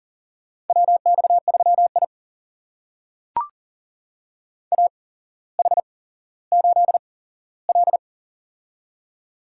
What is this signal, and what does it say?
Morse code, heard as a keyed tone, about 40 wpm, spaced as for 17 wpm, 700 Hz. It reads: WX3I AH8L
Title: USA Call Signs - 8x Character Spacing - 40wpm